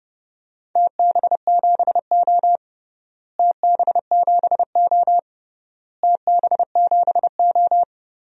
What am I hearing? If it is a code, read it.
Morse code, heard as a keyed tone, about 30 wpm, 700 Hz. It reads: TB7O TB7O TB7O
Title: DXCC-100 Call Signs - 4 Characters - Repeated 3x - 30wpm